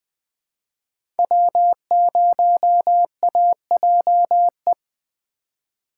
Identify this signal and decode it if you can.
Morse code, heard as a keyed tone, about 20 wpm, 700 Hz. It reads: W0AJE
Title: Speed Racing - USA 1x3 Call Signs 17wpm